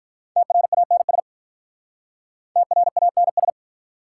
Morse code, heard as a keyed tone, about 44 wpm, 700 Hz. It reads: TRANS TRANS